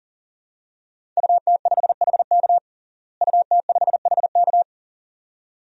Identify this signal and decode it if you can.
Morse code, heard as a keyed tone, about 40 wpm, 700 Hz. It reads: UT5HK UT5HK
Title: DXCC-100 Call Signs - 5 Characters - Repeated 2x - 40wpm